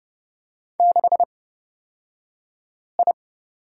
Morse code, heard as a keyed tone, about 30 wpm, spaced as for 5 wpm, 700 Hz. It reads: 6I
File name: Lesson 37d - Letter x - Two Characters 30wpm